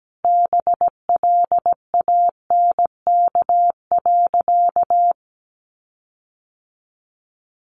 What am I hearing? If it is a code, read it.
Morse code, heard as a keyed tone, about 17 wpm, 700 Hz. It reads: BLANK.
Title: Sentences from Top 1500 Words 17wpm